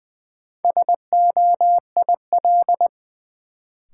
Morse code, heard as a keyed tone, about 20 wpm, 700 Hz. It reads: SOIL
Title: Sentences from Top 1000 Words - Review Part 4 - New Word Drill - 20wpm